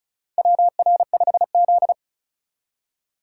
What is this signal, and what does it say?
Morse code, heard as a keyed tone, about 34 wpm, 700 Hz. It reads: WR5Z
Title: USA Call Signs - 4 Characters - 3x Character Spacing - 35wpm